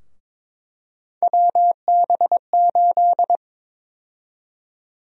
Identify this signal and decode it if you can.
Morse code, heard as a keyed tone, about 22 wpm, 700 Hz. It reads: WB8
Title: USA Call Sign Prefixes - Repeated 2x - 22wpm